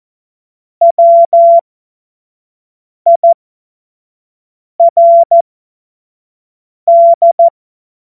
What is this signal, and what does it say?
Morse code, heard as a keyed tone, about 14 wpm, spaced as for 5 wpm, 680 Hz. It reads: WIRD
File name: Morsekurs / Witze / Witz 22. August 2023 25 F70